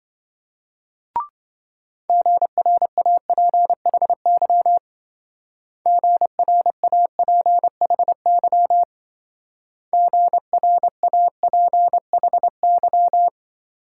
Morse code, heard as a keyed tone, about 25 wpm, 700 Hz. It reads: GRAPHY GRAPHY GRAPHY